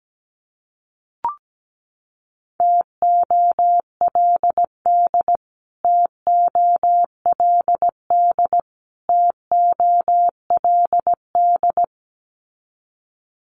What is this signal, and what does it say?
Morse code, heard as a keyed tone, about 17 wpm, 700 Hz. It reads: TOLD TOLD TOLD
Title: Top 500 Words Repeated 17wpm